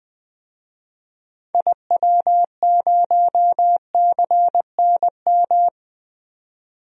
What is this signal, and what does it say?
Morse code, heard as a keyed tone, about 20 wpm, 700 Hz. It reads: IW0CNM
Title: DXCC-100 Call Signs - 6 Characters - 20wpm